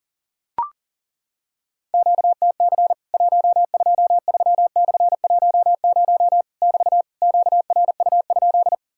Morse code, heard as a keyed tone, about 40 wpm, 700 Hz. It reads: QTC 123/10 <BT> QRU?
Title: Worked All Europe DX Contest - QTC Exchange - 40wpm